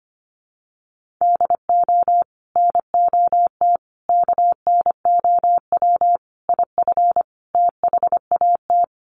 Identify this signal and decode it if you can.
Morse code, heard as a keyed tone, about 25 wpm, 700 Hz. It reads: DO NOT KNOW IF THAT
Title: Sets of 5 Words Encore2 25wpm